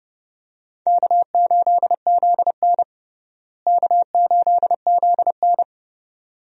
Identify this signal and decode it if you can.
Morse code, heard as a keyed tone, about 32 wpm, 700 Hz. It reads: K8ZN K8ZN
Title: USA Call Signs - 4 Characters - Repeated 2x - 30wpm